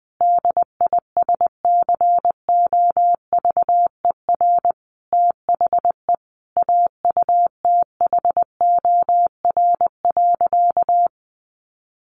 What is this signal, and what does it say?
Morse code, heard as a keyed tone, about 20 wpm, 700 Hz. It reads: DISCOVER THE AUTHOR.
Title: Sentences from Top 1200 Words 20wpm